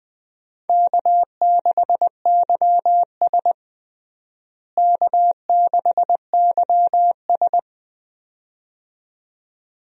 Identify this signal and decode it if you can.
Morse code, heard as a keyed tone, about 20 wpm, 700 Hz. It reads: K6YS K6YS
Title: USA Call Signs - 4 Characters - Repeated 2x - 20wpm